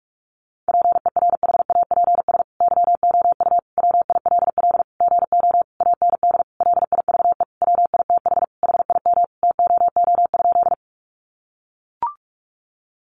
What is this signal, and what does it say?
Morse code, heard as a keyed tone, about 44 wpm, 700 Hz. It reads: PERHAPS YOU WILL GO AND LIVE WITH HIM TOO?
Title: Sentences from Top 500 Words 45wpm